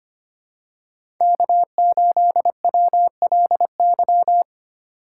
Morse code, heard as a keyed tone, about 25 wpm, 700 Hz. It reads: K8WLY
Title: USA Call Signs - 5 Characters - 8x Character Spacing - 25wpm